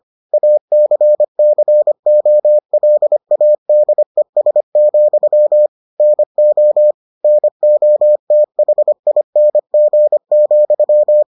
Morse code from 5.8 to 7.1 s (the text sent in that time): NO